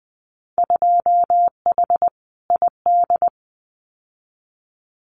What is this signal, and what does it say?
Morse code, heard as a keyed tone, about 20 wpm, 700 Hz. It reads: 2H ID
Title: Winter Field Day Exchanges 20wpm